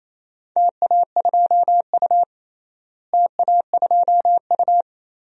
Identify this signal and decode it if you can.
Morse code, heard as a keyed tone, about 28 wpm, 700 Hz. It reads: TA2U TA2U
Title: DXCC-100 Call Signs - 4 Characters - Repeated 2x - 28wpm